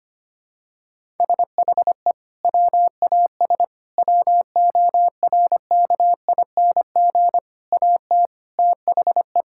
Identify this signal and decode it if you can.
Morse code, heard as a keyed tone, about 25 wpm, 700 Hz. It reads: SHE WAS WORKING AT THE
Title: Sentences from Top 1000 Words - Review Part 5 - 25wpm